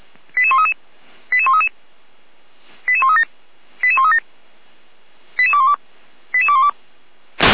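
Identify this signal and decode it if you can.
Five-tone selective call (ZVEI), heard as quick runs of five tones, tones of about 70 ms. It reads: 88133 88133 88137 88137 88212 88212
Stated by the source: ZVEI five-tone call 88213, received by SDR.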